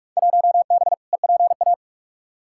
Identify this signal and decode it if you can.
Morse code, heard as a keyed tone, about 47 wpm, 700 Hz. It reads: 1B EPA
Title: Field Day Exchanges - 45wpm